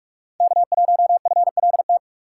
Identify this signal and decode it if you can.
Morse code, heard as a keyed tone, about 45 wpm, 700 Hz. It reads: K1FLT
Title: USA Call Signs - 5 Characters - 6x Character Spacing - 45wpm